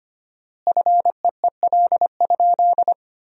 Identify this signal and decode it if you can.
Morse code, heard as a keyed tone, about 25 wpm, 700 Hz. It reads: FEEL?
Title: 4x Word-Spacing - Sentences from Top 400 Words - 25wpm